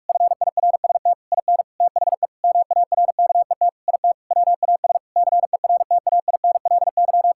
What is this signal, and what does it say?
Morse code, heard as a keyed tone, about 41 wpm, 700 Hz. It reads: FIRST IN THE MARKET IT WAS CERTAINLY
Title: Sentences from Top 1600 Words 45wpm